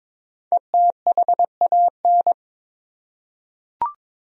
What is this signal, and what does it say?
Morse code, heard as a keyed tone, about 22 wpm, 700 Hz. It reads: ETHAN
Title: USA Common Names - Max 5 Letters - 22wpm